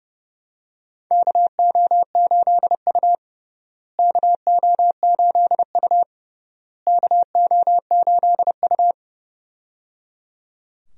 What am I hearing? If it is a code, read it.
Morse code, heard as a keyed tone, about 28 wpm, 700 Hz. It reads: KO8U KO8U KO8U
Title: USA Call Signs - 4 Characters - Repeated 3x - 30wpm